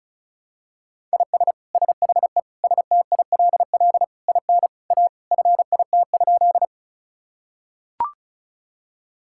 Morse code, heard as a keyed tone, about 36 wpm, 700 Hz. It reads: IS SHE STILL IN A FIT?
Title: Sentences from Top 1000 Words - Review Part 5 - 35wpm